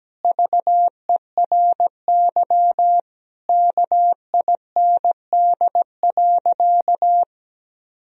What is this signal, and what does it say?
Morse code, heard as a keyed tone, about 17 wpm, 700 Hz. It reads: VERY KIND.
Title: Sentences from Top 1000 Words - Review Part 3 - 17wpm